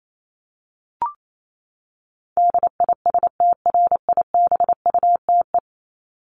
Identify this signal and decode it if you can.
Morse code, heard as a keyed tone, about 28 wpm, 700 Hz. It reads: DISTRIBUTE
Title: Top 1501-1750 Words 28wpm